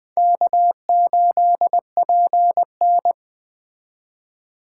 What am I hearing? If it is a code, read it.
Morse code, heard as a keyed tone, about 21 wpm, 700 Hz. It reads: K8PN